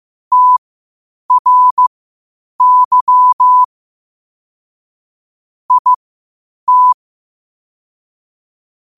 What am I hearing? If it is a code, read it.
Morse code, heard as a keyed tone, about 15 wpm, spaced as for 8 wpm, 1000 Hz. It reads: TRY IT
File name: 8wpm